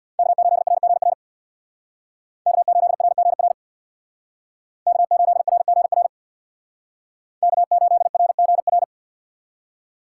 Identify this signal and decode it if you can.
Morse code, heard as a keyed tone, about 63 wpm, 700 Hz. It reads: K8RGR K8RGR K8RGR K8RGR